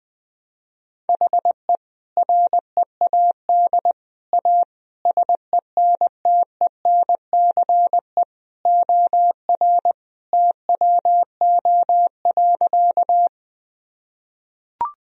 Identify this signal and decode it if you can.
Morse code, heard as a keyed tone, about 20 wpm, 700 Hz. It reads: HE READ A SENTENCE OR TWO.
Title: Sentences from Top 1000 Words - Review Part 1 - 20wpm